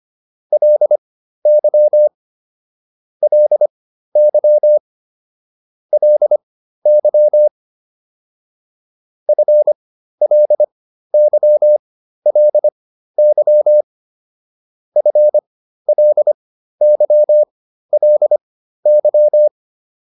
Morse code, heard as a keyed tone, about 25 wpm, spaced as for 13 wpm, 600 Hz. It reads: LY LY LY FLYLY FLYLY
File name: LY5 13